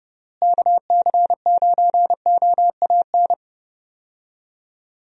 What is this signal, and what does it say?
Morse code, heard as a keyed tone, about 30 wpm, 700 Hz. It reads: KC9OAN
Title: USA Call Signs - 6 Characters - Repeated 3x - 30wpm